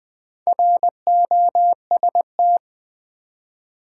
Morse code, heard as a keyed tone, about 20 wpm, 700 Hz. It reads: ROST
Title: Build A Word - 2 Letters At A Time - 6-Letter Words - 20wpm